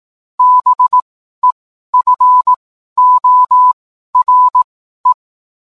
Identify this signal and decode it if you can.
Morse code, heard as a keyed tone, about 18 wpm, spaced as for 12 wpm, 1000 Hz. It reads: BEFORE